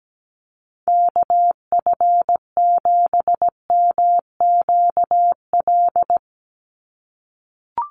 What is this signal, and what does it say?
Morse code, heard as a keyed tone, about 17 wpm, 700 Hz. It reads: KF7MQL
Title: USA Call Signs 17wpm